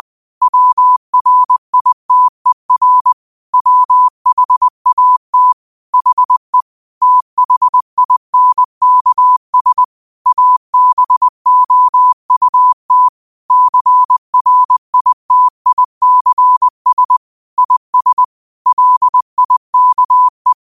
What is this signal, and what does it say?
Morse code, heard as a keyed tone, about 20 wpm, 1000 Hz. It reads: WRITER WHAT HE THINKS ABOUT CRITICS IS LIKE
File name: qotdfiles / 20wpm